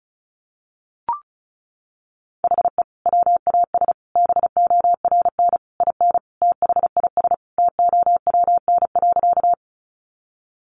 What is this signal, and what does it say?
Morse code, heard as a keyed tone, about 34 wpm, 700 Hz. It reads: HE WAS BORN IN THIS TOWN.
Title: Sentences from Top 1000 Words 35wpm